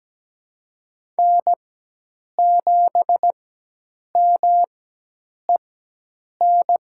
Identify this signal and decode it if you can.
Morse code, heard as a keyed tone, about 17 wpm, spaced as for 8 wpm, 700 Hz. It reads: N7MEN
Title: USA Call Signs - 5 Characters - 4x Character Spacing - 17wpm